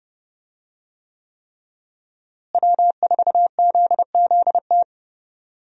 Morse code, heard as a keyed tone, about 30 wpm, 700 Hz. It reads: W4ZZT
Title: USA Call Signs - 5 Characters - Repeated 3x - 30wpm